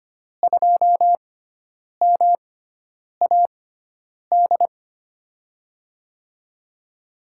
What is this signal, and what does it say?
Morse code, heard as a keyed tone, about 25 wpm, spaced as for 9 wpm, 700 Hz. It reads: 2MAD
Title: USA Call Signs - 5 Characters - 6x Character Spacing - 25wpm